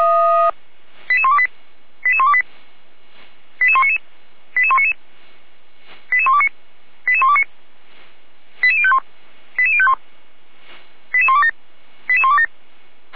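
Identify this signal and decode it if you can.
Five-tone selective call (ZVEI), heard as quick runs of five tones, tones of about 70 ms. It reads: 88128 88128 88199 88199 88129 88129 88051 88051 88127 88127
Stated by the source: ZVEI five-tone call received by SDR.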